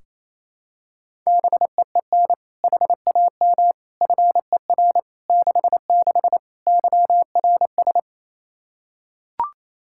Morse code, heard as a keyed tone, about 28 wpm, 700 Hz. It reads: BEEN HAM FER 66 YRS
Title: Practice QSO - Exchange 3 - More Info - 28wpm